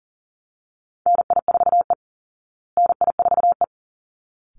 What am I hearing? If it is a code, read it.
Morse code, heard as a keyed tone, about 40 wpm, 700 Hz. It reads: NI4E NI4E
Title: Call Signs Repeated 3x - 40wpm